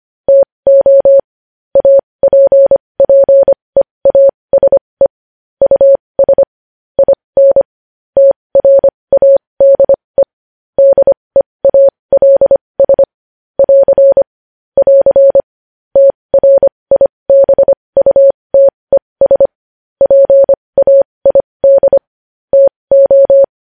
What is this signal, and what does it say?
Morse code, heard as a keyed tone, about 25 wpm, spaced as for 20 wpm, 550 Hz. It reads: TO APPEASE US IN TRADE DEALS <AR> <AR> TRIBUTES PAID TO
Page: News Headlines in Morse Code at 20 WPM